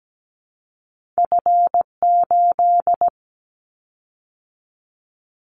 Morse code, heard as a keyed tone, about 17 wpm, 700 Hz. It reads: F8
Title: Callsign Prefixes for Top 25 DXCC - 17wpm